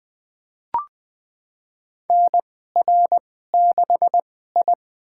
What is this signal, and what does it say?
Morse code, heard as a keyed tone, about 20 wpm, spaced as for 14 wpm, 700 Hz. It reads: NR6I
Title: USA Call Signs - 4 Characters - 2x Character Spacing - 20wpm